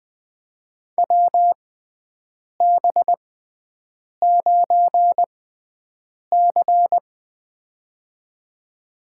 Morse code, heard as a keyed tone, about 20 wpm, spaced as for 7 wpm, 700 Hz. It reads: WB9C
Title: USA Call Signs - 4 Characters - 6x Character Spacing - 20wpm